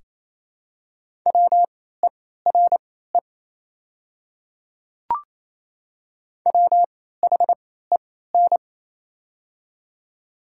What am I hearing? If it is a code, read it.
Morse code, heard as a keyed tone, about 27 wpm, spaced as for 16 wpm, 700 Hz. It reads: WERE WHEN
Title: 3x Character Spacing - Top 100 Words - 28wpm